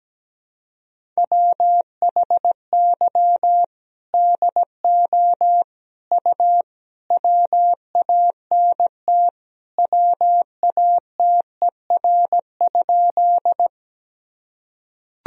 Morse code, heard as a keyed tone, about 17 wpm, 700 Hz. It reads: WHY DO U WANT WATER?
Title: Sentences from Top 200 Words and Abbreviations 17wpm